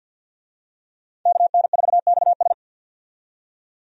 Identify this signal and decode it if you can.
Morse code, heard as a keyed tone, about 48 wpm, 700 Hz. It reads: KN4XS
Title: USA Call Signs - 5 Characters - 50wpm